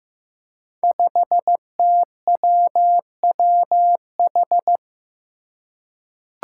Morse code, heard as a keyed tone, about 15 wpm, 700 Hz. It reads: 5TWWH
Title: Five Challenging Letter-Number Combinations 15wpm